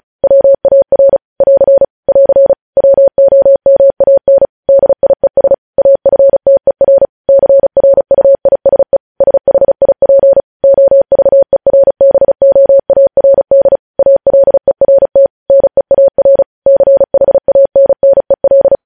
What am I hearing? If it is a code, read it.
Morse code, heard as a keyed tone, about 36 wpm, 550 Hz. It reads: WAR <AR> <AR> WOMAN DIES AFTER CRUISE SHIP OVERBOARD ALERT NEAR CHANNEL